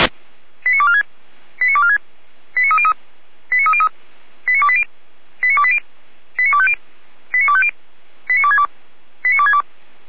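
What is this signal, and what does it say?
Five-tone selective call (ZVEI), heard as quick runs of five tones, tones of about 70 ms. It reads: 89246 89246 89303 89303 89280 89280 89250 89250 89262 89262
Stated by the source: ZVEI five-tone selective call received by SDR, from the Feuerwehr Orlamünde (fire brigade) audio archive.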